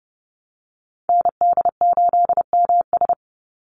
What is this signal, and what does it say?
Morse code, heard as a keyed tone, about 30 wpm, 700 Hz. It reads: ND8MS
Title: Call Signs Repeated 3x - 30wpm